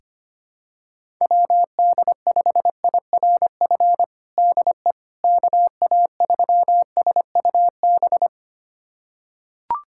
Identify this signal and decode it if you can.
Morse code, heard as a keyed tone, about 25 wpm, 700 Hz. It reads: WD5IRF DE KA3SUB
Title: USA Callsign DE Callsign - 6 Characters - 25wpm